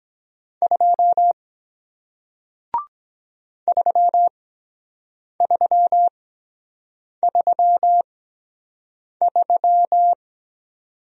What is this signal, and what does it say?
Morse code, heard as a keyed tone, about 23 wpm, spaced as for 7 wpm, 700 Hz. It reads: 2 3333